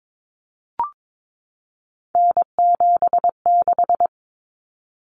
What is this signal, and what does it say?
Morse code, heard as a keyed tone, about 22 wpm, 700 Hz. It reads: N76
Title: Common Cut Numbers - 3-Digit 22wpm